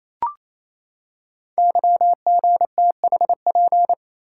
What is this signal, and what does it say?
Morse code, heard as a keyed tone, about 29 wpm, 700 Hz. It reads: YGTHP
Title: Code Groups - 5 Letters - 28wpm